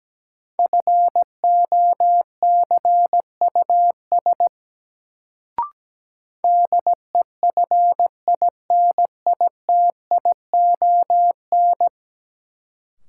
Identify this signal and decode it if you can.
Morse code, heard as a keyed tone, about 17 wpm, 700 Hz. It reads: FOCUS DEFINITION
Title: Sentences from Top 1000 Words - Review Part 4 - New Word Drill - 17wpm